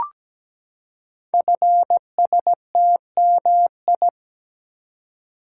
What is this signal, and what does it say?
Morse code, heard as a keyed tone, about 17 wpm, 700 Hz. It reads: FSTMI
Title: Code Groups - 5 Letters - 17wpm